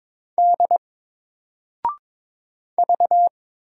Morse code, heard as a keyed tone, about 22 wpm, 700 Hz. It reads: D V
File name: Single-Letter-Number-22wpm